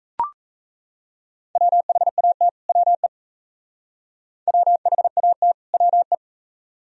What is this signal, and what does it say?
Morse code, heard as a keyed tone, about 34 wpm, 700 Hz. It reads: WHAT WE WHAT WE
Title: Speed-Racing - Sets of 2 Words 28wpm